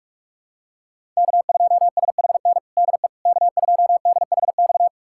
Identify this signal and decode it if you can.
Morse code, heard as a keyed tone, about 46 wpm, 700 Hz. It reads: K2SHN DE K2DHX
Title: USA Callsign DE Callsign - 5 Characters - 45wpm